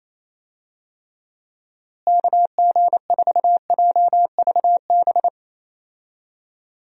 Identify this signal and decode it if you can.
Morse code, heard as a keyed tone, about 27 wpm, 700 Hz. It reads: KG4JVB
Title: USA Call Signs - 6 Characters - Repeated 2x - 28wpm